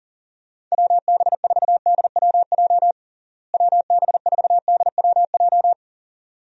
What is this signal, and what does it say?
Morse code, heard as a keyed tone, about 40 wpm, 700 Hz. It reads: WB4DWJ WB4DWJ
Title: USA Call Signs - 6 Characters - Repeated 2x - 40wpm